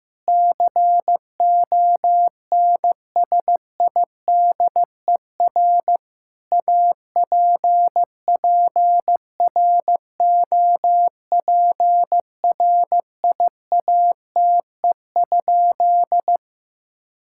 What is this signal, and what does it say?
Morse code, heard as a keyed tone, about 16 wpm, 700 Hz. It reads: CONSIDER APPROPRIATE?
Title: Sentences from Top 1000 Words - Review Part 5 - 15wpm